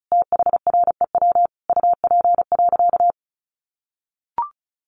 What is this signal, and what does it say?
Morse code, heard as a keyed tone, about 34 wpm, 700 Hz. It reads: THREW UP.
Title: A Sound Of Thunder 35wpm